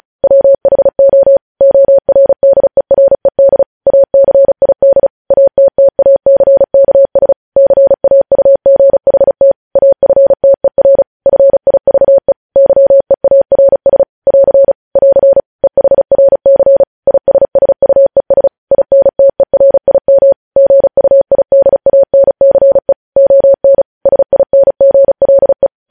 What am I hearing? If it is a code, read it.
Morse code, heard as a keyed tone, about 36 wpm, 550 Hz. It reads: WHO ORDERED ACID ATTACKS CAUGHT AFTER FIVE YEARS <AR> <AR> EHRC ISSUES INTERIM GUIDANCE ON SINGLE
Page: News headlines in Morse code at 35 wpm